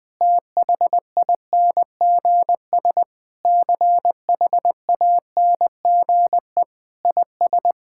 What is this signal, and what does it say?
Morse code, heard as a keyed tone, about 21 wpm, 700 Hz. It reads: THINGS CHANGE IS
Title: Sentences with 8-Word Phrases - Spoken Beginning - 20wpm